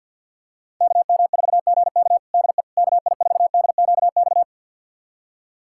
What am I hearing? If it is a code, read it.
Morse code, heard as a keyed tone, about 51 wpm, 700 Hz. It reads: KM4GK DE KI4DQX
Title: USA Callsign DE Callsign 50wpm